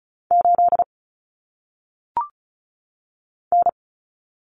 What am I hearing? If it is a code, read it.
Morse code, heard as a keyed tone, about 35 wpm, 700 Hz. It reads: 8 N
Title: Common Cut Numbers - 1-Digit 35wpm